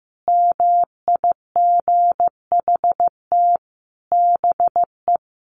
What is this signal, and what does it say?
Morse code, heard as a keyed tone, about 15 wpm, 700 Hz. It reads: MIGHT BE
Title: Sentences from Top 1400 Words 15wpm